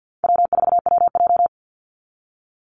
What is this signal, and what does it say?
Morse code, heard as a keyed tone, about 48 wpm, 700 Hz. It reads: W4WJ
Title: Call Signs Repeated 2x - 50wpm